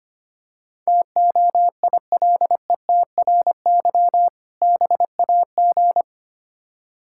Morse code, heard as a keyed tone, about 26 wpm, 700 Hz. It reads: TOILETRY BAG
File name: practice / 25wpm / Difficult Words - Speed-Racing with Hints - 25wpm